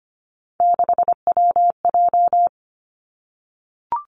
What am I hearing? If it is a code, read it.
Morse code, heard as a keyed tone, about 25 wpm, 700 Hz. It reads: 6WJ
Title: Three Challenging Letter-Number Combinations 25wpm